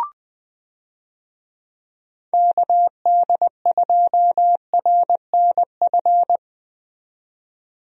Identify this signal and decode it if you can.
Morse code, heard as a keyed tone, about 21 wpm, 700 Hz. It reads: KD2RNF
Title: USA Call Signs - 6 Characters - 20wpm